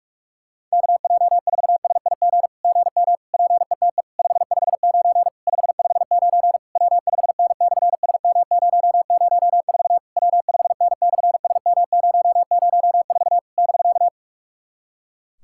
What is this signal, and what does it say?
Morse code, heard as a keyed tone, about 46 wpm, 700 Hz. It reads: KJ4SIG GM PETE 559 559 W5N/SM004 W5N/SM004 <BK>
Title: SOTA QSO Practice 45wpm